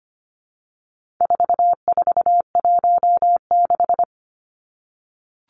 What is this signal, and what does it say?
Morse code, heard as a keyed tone, about 25 wpm, 700 Hz. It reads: 4416
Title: Numbers - 4-Digit 25wpm